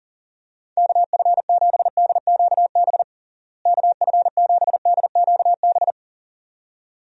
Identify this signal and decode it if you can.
Morse code, heard as a keyed tone, about 40 wpm, 700 Hz. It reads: KF7DQB KF7DQB
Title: USA Call Signs - 6 Characters - Repeated 2x - 40wpm